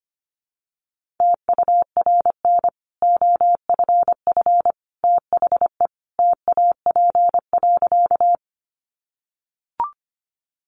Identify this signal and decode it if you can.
Morse code, heard as a keyed tone, about 25 wpm, 700 Hz. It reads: TURN OFF THE TAP.
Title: Sentences from Top 1400 Words 25wpm